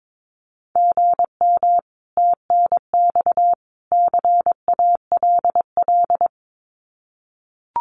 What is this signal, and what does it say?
Morse code, heard as a keyed tone, about 22 wpm, 700 Hz. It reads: GM TNX CALL
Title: Common QSO Phrases 22wpm